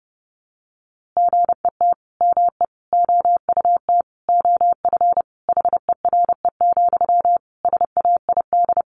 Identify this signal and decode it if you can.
Morse code, heard as a keyed tone, about 30 wpm, 700 Hz. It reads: GET ME OUT OF HERE, SAID
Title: A Sound Of Thunder 30wpm